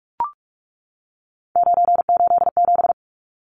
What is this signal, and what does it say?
Morse code, heard as a keyed tone, about 44 wpm, 700 Hz. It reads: 987
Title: Numbers - 3-Digit 45wpm